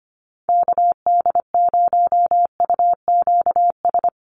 Morse code, heard as a keyed tone, about 25 wpm, 700 Hz. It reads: KD0UQS
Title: USA Call Signs Repeated 3x 25wpm